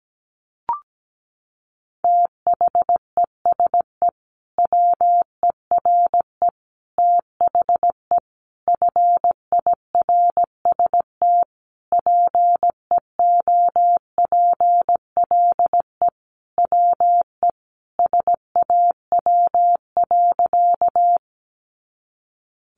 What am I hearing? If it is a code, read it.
Morse code, heard as a keyed tone, about 17 wpm, 700 Hz. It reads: THESE WERE THE FIRST PEOPLE WE SAW.
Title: Sentences from Top 300 Words 17wpm